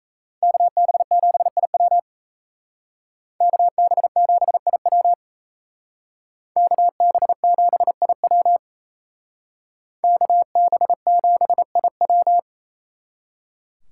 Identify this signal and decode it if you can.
Morse code, heard as a keyed tone, about 36 wpm, 700 Hz. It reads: KB7IW KB7IW KB7IW KB7IW